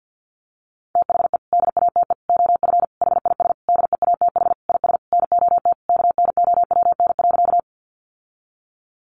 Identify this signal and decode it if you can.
Morse code, heard as a keyed tone, about 48 wpm, 700 Hz. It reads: THE DATE OF HIS DEATH IS NOT KNOWN.
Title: Sentences from Top 1100 Words 50wpm